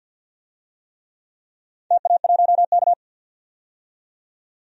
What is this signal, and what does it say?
Morse code, heard as a keyed tone, about 52 wpm, 700 Hz. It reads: TA1K
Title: DXCC-100 Call Signs - 4 Characters - 50wpm